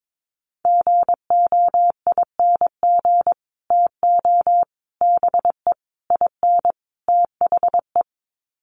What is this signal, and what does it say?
Morse code, heard as a keyed tone, about 22 wpm, 700 Hz. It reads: GOING TO BE IN THE
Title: Sets of 5 Words Encore 22wpm